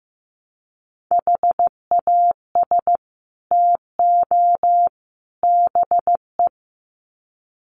Morse code, heard as a keyed tone, about 15 wpm, 700 Hz. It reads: HAS TO BE
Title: Sets of 3 Words 15wpm